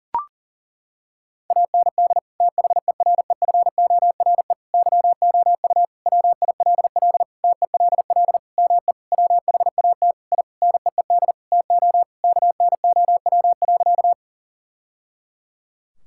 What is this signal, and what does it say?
Morse code, heard as a keyed tone, about 40 wpm, 700 Hz. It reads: AND THEREFORE YOU WILL TELL ME WHAT I NEED TO KNOW.
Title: Sentences from Top 1000 Words - Review Part 5 - 40wpm